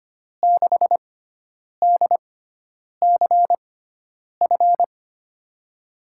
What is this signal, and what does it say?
Morse code, heard as a keyed tone, about 25 wpm, spaced as for 9 wpm, 700 Hz. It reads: 6DCF
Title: USA Call Signs - 5 Characters - 6x Character Spacing - 25wpm